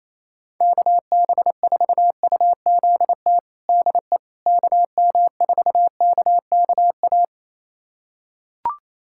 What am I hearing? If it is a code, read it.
Morse code, heard as a keyed tone, about 28 wpm, 700 Hz. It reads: KB4UZT DE KM4KKA
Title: USA Callsign DE Callsign - 6 Characters - 28wpm